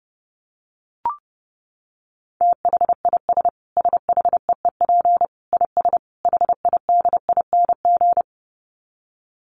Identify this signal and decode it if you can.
Morse code, heard as a keyed tone, about 30 wpm, 700 Hz. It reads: THIS SHEEP IS HIDING
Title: Dit Challenge 30wpm